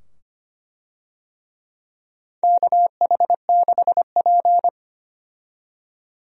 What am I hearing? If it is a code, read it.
Morse code, heard as a keyed tone, about 25 wpm, 700 Hz. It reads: KH6P
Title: DXCC-100 Call Signs - 4 Characters - 25wpm